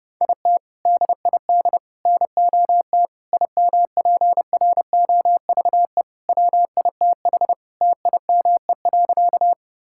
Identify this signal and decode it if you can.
Morse code, heard as a keyed tone, about 32 wpm, 700 Hz. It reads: IT DID NOT IMPROVE WITH TIME.
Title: Sentences from Top 1000 Words - Review Part 5 - 30wpm